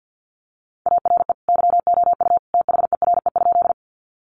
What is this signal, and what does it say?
Morse code, heard as a keyed tone, about 50 wpm, 700 Hz. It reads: ARE YOU THERE?
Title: Sentences from Top 100 Words 50wpm